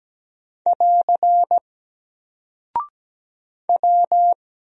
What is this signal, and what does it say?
Morse code, heard as a keyed tone, about 17 wpm, 700 Hz. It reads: <AR> W
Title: Single Character (Letter:Number:Punctuation:Prosign) 17wpm